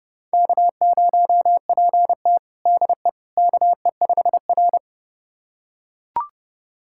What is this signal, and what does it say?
Morse code, heard as a keyed tone, about 30 wpm, 700 Hz. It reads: K0PT DE KE5R
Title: USA Callsign DE Callsign - 4 Characters - 30wpm